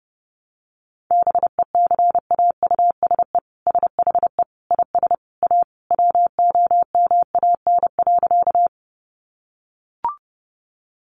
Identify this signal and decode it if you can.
Morse code, heard as a keyed tone, about 30 wpm, 700 Hz. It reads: BECAUSE SHE IS A WOMAN.
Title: Sentences from Top 1100 Words 30wpm